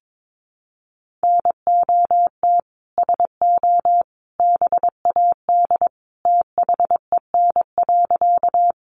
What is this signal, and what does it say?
Morse code, heard as a keyed tone, about 22 wpm, 700 Hz. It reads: NOT SO BAD THEN.
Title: Sentences from Top 700 Words 22wpm